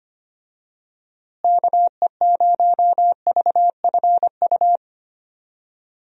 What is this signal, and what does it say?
Morse code, heard as a keyed tone, about 25 wpm, 700 Hz. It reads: KE0VFU
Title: USA Call Signs - 6 Characters - Repeated 2x - 25wpm